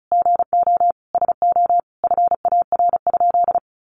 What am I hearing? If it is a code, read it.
Morse code, heard as a keyed tone, about 34 wpm, 700 Hz. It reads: GO SO FAR?
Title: Sentences from Top 300 Words 35wpm